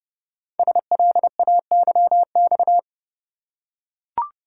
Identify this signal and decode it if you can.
Morse code, heard as a keyed tone, about 30 wpm, 700 Hz. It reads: SLAYX
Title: Code Groups - 5 Letters - 30wpm